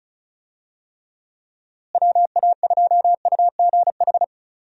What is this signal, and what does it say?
Morse code, heard as a keyed tone, about 34 wpm, 700 Hz. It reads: WA2UGH
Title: USA Call Signs - 6 Characters - Repeated 2x - 35wpm